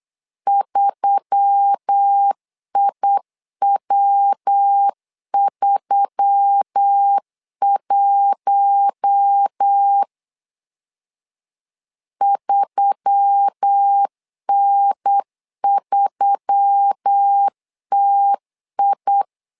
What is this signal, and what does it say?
Morse code, heard as a keyed tone, about 8 wpm, 800 Hz. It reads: 3IW31 3N3TI
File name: LEZIONI / 4 Q1S3 8wpm 3037